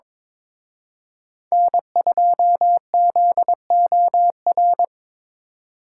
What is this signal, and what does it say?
Morse code, heard as a keyed tone, about 26 wpm, 700 Hz. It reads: N2ZOR
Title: Speed Racing - USA Call Signs 22wpm